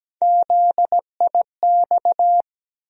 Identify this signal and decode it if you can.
Morse code, heard as a keyed tone, about 17 wpm, 700 Hz. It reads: ZIX
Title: USA Call Sign Postfixes - 3 Letters 17wpm